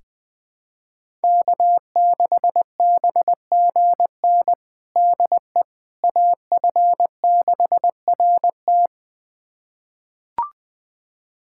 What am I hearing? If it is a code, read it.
Morse code, heard as a keyed tone, about 21 wpm, 700 Hz. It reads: K6BGN DE AF6RT
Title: USA Callsign DE Callsign - 5 Characters - 20wpm